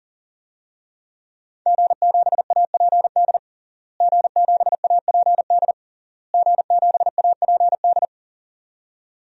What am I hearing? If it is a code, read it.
Morse code, heard as a keyed tone, about 37 wpm, 700 Hz. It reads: G7APD G7APD G7APD
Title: DXCC-100 Call Signs - 5 Characters - Repeated 3x - 40wpm